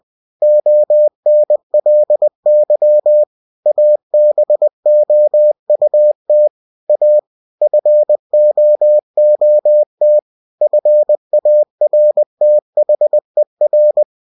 Morse code, heard as a keyed tone, about 20 wpm, 600 Hz. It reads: ONLY ABOUT A FOOT FARTHER
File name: cw-books / winnie-the-pooh / winnie-20wpm-page-0036